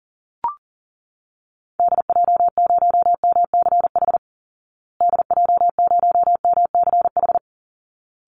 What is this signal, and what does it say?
Morse code, heard as a keyed tone, about 40 wpm, 700 Hz. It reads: DJ0MCH DJ0MCH
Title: Call Signs Repeated 3x - 40wpm